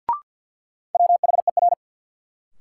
Morse code, heard as a keyed tone, about 53 wpm, 700 Hz. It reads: WHER
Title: Sentences from Top 1500 Words - New Word Drill - 50wpm